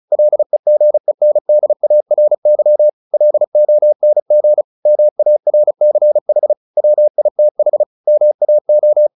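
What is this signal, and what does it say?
Morse code, heard as a keyed tone, about 36 wpm, 600 Hz. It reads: LEGENDARY LONG MARCH WITH MAO
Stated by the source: Morse practice at 35 wpm from CW med Gnister, sending Ken Follett's novel Never in English.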